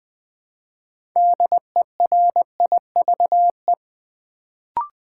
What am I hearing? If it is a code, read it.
Morse code, heard as a keyed tone, about 20 wpm, 700 Hz. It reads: DERIVE
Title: Build A Word - 3 Letters At A Time - 6-Letter Words - 20wpm